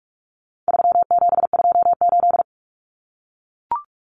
Morse code, heard as a keyed tone, about 44 wpm, 700 Hz. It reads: 3728
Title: Common Cut Numbers - 4-Digit 45wpm